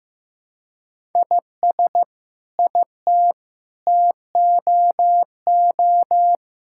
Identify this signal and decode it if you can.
Morse code, heard as a keyed tone, about 15 wpm, 700 Hz. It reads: IS IT TOO
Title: Sentences from Top 1000 Words - Review Part 6 - 15wpm